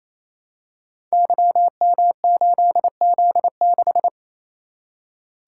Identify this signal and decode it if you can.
Morse code, heard as a keyed tone, about 27 wpm, 700 Hz. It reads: YM8Z6
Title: Code Groups - 5 Letter:Numbers - 28wpm